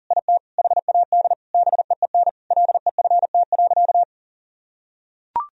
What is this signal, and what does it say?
Morse code, heard as a keyed tone, about 40 wpm, 700 Hz. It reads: IT HAD BEEN LEFT.
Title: Sentences from Top 1000 Words - Review Part 2 - 40wpm